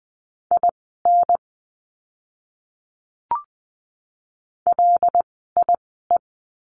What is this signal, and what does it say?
Morse code, heard as a keyed tone, about 20 wpm, spaced as for 15 wpm, 700 Hz. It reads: IN LIE
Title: 094 Lesson 16f - Letter c - Words 20wpm